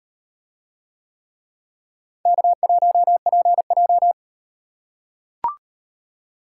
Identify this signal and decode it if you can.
Morse code, heard as a keyed tone, about 38 wpm, 700 Hz. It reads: K1PJ